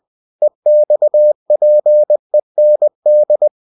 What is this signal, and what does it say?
Morse code, heard as a keyed tone, about 20 wpm, 600 Hz. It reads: EXPEND